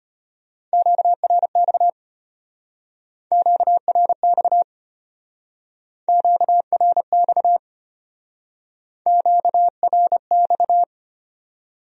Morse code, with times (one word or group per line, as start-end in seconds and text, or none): 0.7-1.9 QRX
3.3-4.6 QRX
6.1-7.6 QRX
9.1-10.8 QRX